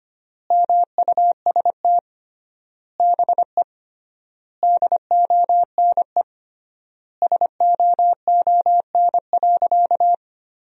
Morse code, heard as a keyed tone, about 25 wpm, 700 Hz. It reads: MUST BE DONE SOON.
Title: 3x Word-Spacing - Sentences from Top 500 Words - 25wpm